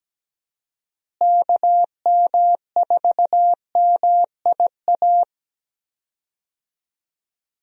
Morse code, heard as a keyed tone, about 17 wpm, 700 Hz. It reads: KM4MIA
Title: USA Call Signs - 6 Characters - Repeated 3x - 17wpm